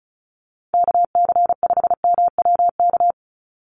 Morse code, heard as a keyed tone, about 34 wpm, 700 Hz. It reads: KC5MWK
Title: USA Call Signs 35wpm